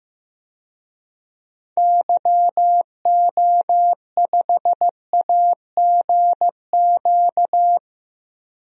Morse code, heard as a keyed tone, about 15 wpm, 700 Hz. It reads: YO5AGQ
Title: DXCC-100 Call Signs - 6 Characters - Repeated 2x - 15wpm